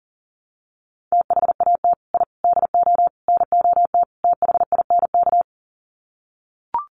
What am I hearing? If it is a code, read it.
Morse code, heard as a keyed tone, about 40 wpm, 700 Hz. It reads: THAT I DO NOT THINK
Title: Sets of 5 Words Encore 40wpm